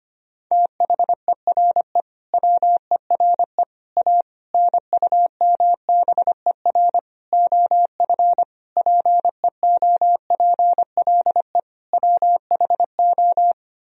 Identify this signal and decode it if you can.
Morse code, heard as a keyed tone, about 25 wpm, 700 Hz. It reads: THERE WERE A NUMBER OF PEOPLE WHO
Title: Sentences from Top 1000 Words - Review Part 1 - 25wpm